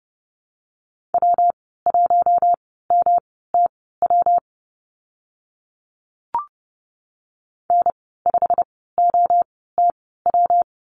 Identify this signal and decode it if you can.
Morse code, heard as a keyed tone, about 30 wpm, spaced as for 17 wpm, 700 Hz. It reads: W1MTW N5OTW
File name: Lesson 19g - Letter w - Call Signs 30wpm